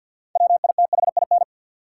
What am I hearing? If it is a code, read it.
Morse code, heard as a keyed tone, about 53 wpm, 700 Hz. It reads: WITHIN
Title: Sets of 2 Words - New Word Drill - 50wpm